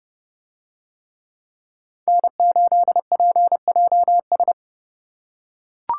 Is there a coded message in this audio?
Yes: Morse code from 2.1 to 4.5 s, heard as a keyed tone, about 31 wpm, 700 Hz. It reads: N8PJS